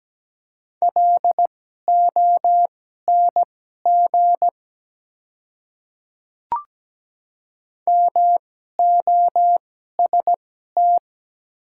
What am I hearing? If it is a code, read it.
Morse code, heard as a keyed tone, about 17 wpm, spaced as for 12 wpm, 700 Hz. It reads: LONG MOST
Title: 2x Character Spacing - Top 100 Words - 17wpm